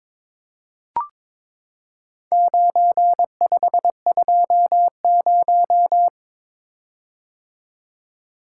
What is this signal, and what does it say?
Morse code, heard as a keyed tone, about 22 wpm, 700 Hz. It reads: 9520
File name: practice / 22wpm / Four Letter-Number Combinations - 22wpm